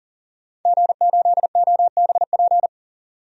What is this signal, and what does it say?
Morse code, heard as a keyed tone, about 39 wpm, 700 Hz. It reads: G8OBP
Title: DXCC-100 Call Signs - 5 Characters - 40wpm